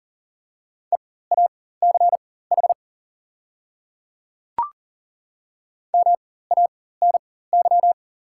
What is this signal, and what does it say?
Morse code, heard as a keyed tone, about 37 wpm, spaced as for 18 wpm, 690 Hz. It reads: EACH MANY
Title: 4x Character Spacing - Top 200 Words - 40wpm